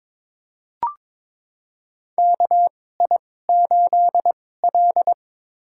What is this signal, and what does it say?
Morse code, heard as a keyed tone, about 22 wpm, spaced as for 16 wpm, 700 Hz. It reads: KI8L
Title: USA Call Signs - 4 Characters - 2x Character Spacing - 22wpm